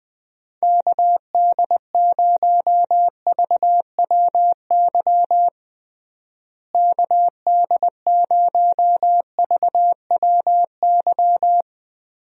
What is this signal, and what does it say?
Morse code, heard as a keyed tone, about 20 wpm, 700 Hz. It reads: KD0VWY KD0VWY
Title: USA Call Signs - 6 Characters - Repeated 2x - 20wpm